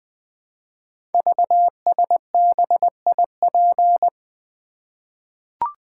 Morse code, heard as a keyed tone, about 20 wpm, 700 Hz. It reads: VSBIP
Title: Code Groups - 5 Letter:Numbers - 20wpm